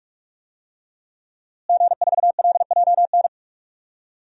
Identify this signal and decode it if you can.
Morse code, heard as a keyed tone, about 42 wpm, 700 Hz. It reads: G4LJN
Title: DXCC-100 Call Signs - 5 Characters - Repeated 2x - 45wpm